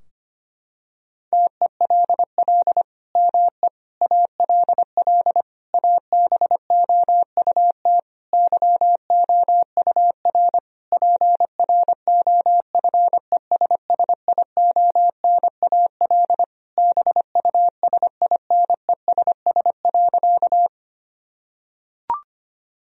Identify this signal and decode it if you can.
Morse code, heard as a keyed tone, about 25 wpm, 700 Hz. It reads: TELL ME ALL ABOUT YOUR PROFESSIONAL BUSINESS.
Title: Sentences from Top 1000 Words - Review Part 4 - 25wpm